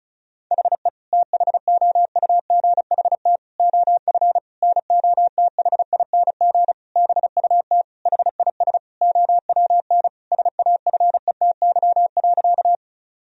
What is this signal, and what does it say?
Morse code, heard as a keyed tone, about 34 wpm, 700 Hz. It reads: HE THOUGHT OF NOTHING BUT HIS OWN SAFETY.
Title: Sentences from Top 1000 Words - Review Part 4 - 35wpm